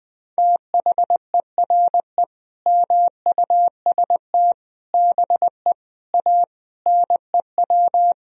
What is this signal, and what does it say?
Morse code, heard as a keyed tone, about 20 wpm, 700 Hz. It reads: THERE MUST BE A NEW
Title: Sentences from Top 1000 Words - Review Part 3 - 20wpm